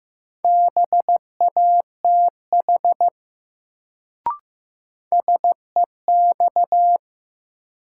Morse code, heard as a keyed tone, about 15 wpm, 700 Hz. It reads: BATH SEX
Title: Sentences from Top 1600 Words - New Word Drill - 15wpm